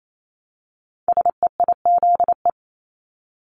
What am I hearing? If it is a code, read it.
Morse code, heard as a keyed tone, about 28 wpm, 700 Hz. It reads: SEIZE
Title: Multi-letter Phonemes Repeated 3x 28wpm